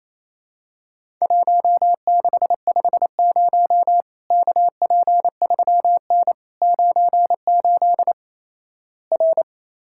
Morse code, heard as a keyed tone, about 28 wpm, 700 Hz. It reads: 1650 KP3N 98 R
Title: Worked All Europe DX Contest - QTC Exchange - 28wpm